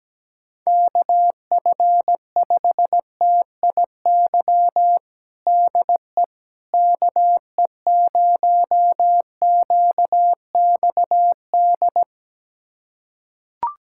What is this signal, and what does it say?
Morse code, heard as a keyed tone, about 17 wpm, 700 Hz. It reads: KF5TIY DE KE0QXD
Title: USA Callsign DE Callsign - 6 Characters - 17wpm